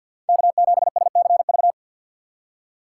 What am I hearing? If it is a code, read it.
Morse code, heard as a keyed tone, about 58 wpm, 700 Hz. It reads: K7SCV
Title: Speed Racing - USA 1x3 Call Signs 50wpm